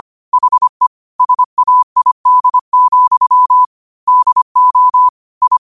Morse code, heard as a keyed tone, about 25 wpm, 1000 Hz. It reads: HE SAID, DO I